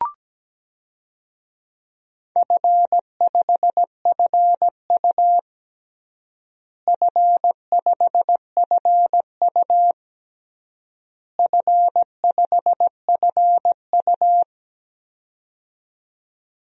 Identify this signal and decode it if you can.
Morse code, heard as a keyed tone, about 17 wpm, 700 Hz. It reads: F5FU F5FU F5FU
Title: DXCC-100 Call Signs - 4 Characters - Repeated 3x - 17wpm